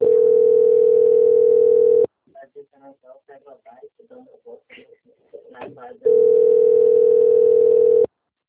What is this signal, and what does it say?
Ringback tone (North America), 440 + 480 Hz, with speech.
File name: out- - - - - .6493